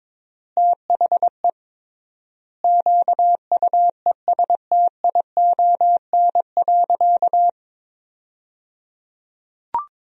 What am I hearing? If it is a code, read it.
Morse code, heard as a keyed tone, about 22 wpm, 700 Hz. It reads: THE QUESTION.
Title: 3x Word-Spacing - Sentences from Top 400 Words - 22wpm